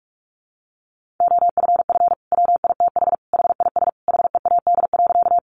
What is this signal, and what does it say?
Morse code, heard as a keyed tone, about 44 wpm, 700 Hz. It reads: OFF WITH HIS HEAD.
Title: Sentences from Top 300 Words 45wpm